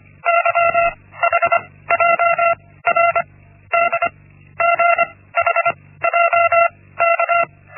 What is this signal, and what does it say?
Morse code, heard as a keyed tone, about 25 wpm, 680 Hz. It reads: Y H J R D G H J K
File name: Linksmakalnis-1